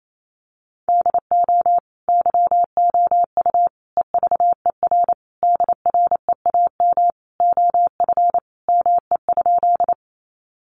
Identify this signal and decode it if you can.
Morse code, heard as a keyed tone, about 28 wpm, 700 Hz. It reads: DO YOU EVER DREAM OF ME?
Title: Sentences from Top 1500 Words 28wpm